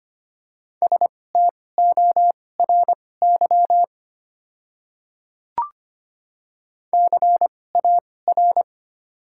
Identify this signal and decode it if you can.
Morse code, heard as a keyed tone, about 25 wpm, spaced as for 18 wpm, 700 Hz. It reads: STORY CAR
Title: 2x Character Spacing - Top 300 Words - 25wpm